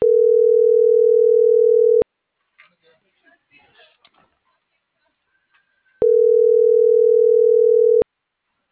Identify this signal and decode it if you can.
Ringback tone (North America), 440 + 480 Hz.